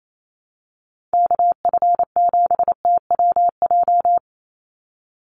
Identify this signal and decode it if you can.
Morse code, heard as a keyed tone, about 28 wpm, 700 Hz. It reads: KF7TWJ
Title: USA Call Signs 28wpm